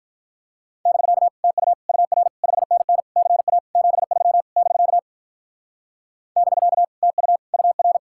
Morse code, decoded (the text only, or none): <BK> TU UR 5NN CA 73 <BK> <BK> TU UR